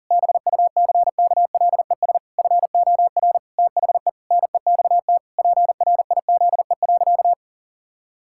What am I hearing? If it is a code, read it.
Morse code, heard as a keyed tone, about 40 wpm, 700 Hz. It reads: BUCKLES FOR THE NEXT PRIZE.
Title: Pangram Challenge v2 - 40wpm